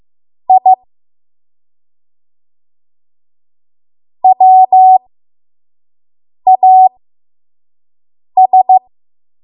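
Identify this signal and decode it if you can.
Morse code, heard as a keyed tone, about 15 wpm, spaced as for 5 wpm, 750 Hz. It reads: I WAS
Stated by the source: 5 WPM Morse code practice.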